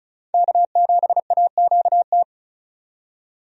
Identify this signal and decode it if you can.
Morse code, heard as a keyed tone, about 34 wpm, 700 Hz. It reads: K7AQT
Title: USA Call Signs - 5 Characters - 35wpm